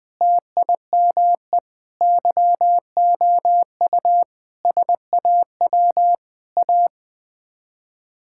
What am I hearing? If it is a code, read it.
Morse code, heard as a keyed tone, about 20 wpm, 700 Hz. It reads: TIME YOU SAW A
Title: Sets of 8 Words - 20wpm